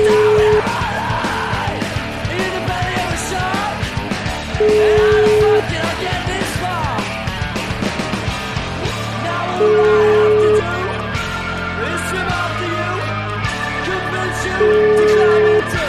Ringback tone (Europe, 425 Hz), with noise.